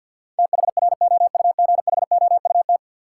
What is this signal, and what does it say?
Morse code, heard as a keyed tone, about 53 wpm, 700 Hz. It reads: THROUGHOUT
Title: Sentences from Top 800 Words - New Word Drill - 50wpm